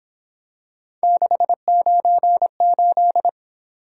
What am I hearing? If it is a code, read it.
Morse code, heard as a keyed tone, about 26 wpm, 700 Hz. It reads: 698
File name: Speed-Racing - Three Letter-Number Combinations - 22wpm